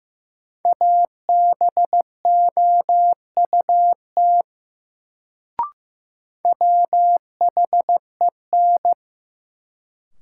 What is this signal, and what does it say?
Morse code, heard as a keyed tone, about 15 wpm, 700 Hz. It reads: ABOUT WHEN
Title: Sentences from Top 100 Words - New Word Drill - 15wpm